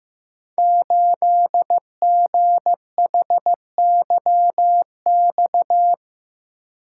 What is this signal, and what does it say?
Morse code, heard as a keyed tone, about 15 wpm, 700 Hz. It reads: 8GHYX
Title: Code Groups - 5 Letter:Numbers - 15wpm